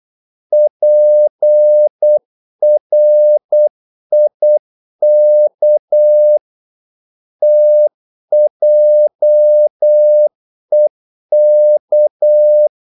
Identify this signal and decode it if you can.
Morse code, heard as a keyed tone, about 8 wpm, 600 Hz. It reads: PRIK TJEK
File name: Korte ord DK 8wpm